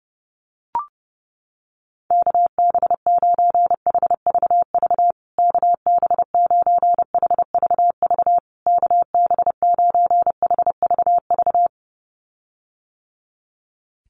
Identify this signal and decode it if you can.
Morse code, heard as a keyed tone, about 30 wpm, 700 Hz. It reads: KB9HVV KB9HVV KB9HVV
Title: USA Call Signs Repeated 3x 30wpm